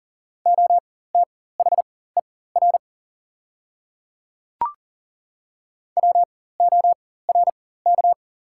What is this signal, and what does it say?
Morse code, heard as a keyed tone, about 37 wpm, spaced as for 18 wpm, 690 Hz. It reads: OTHER WORK
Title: 4x Character Spacing - Top 200 Words - 40wpm